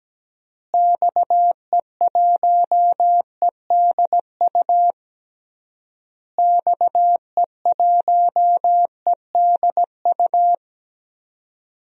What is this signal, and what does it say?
Morse code, heard as a keyed tone, about 17 wpm, 700 Hz. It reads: XE1EDU XE1EDU
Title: DXCC-100 Call Signs - 6 Characters - Repeated 3x - 17wpm